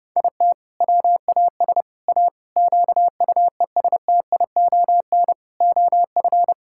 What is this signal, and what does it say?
Morse code, heard as a keyed tone, about 30 wpm, 700 Hz. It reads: IT WAS A QUESTION OF
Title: Sentences from Top 1000 Words - Review Part 6 - 30wpm